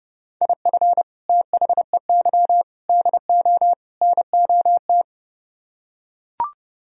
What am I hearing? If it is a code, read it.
Morse code, heard as a keyed tone, about 30 wpm, 700 Hz. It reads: IF THEY DO NOT